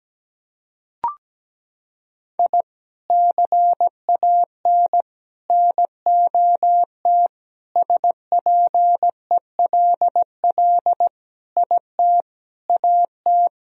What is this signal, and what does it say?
Morse code, heard as a keyed tone, about 17 wpm, 700 Hz. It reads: I CAN NOT SPELL IT AT
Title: Sentences from Top 1000 Words - Review Part 1 - 17wpm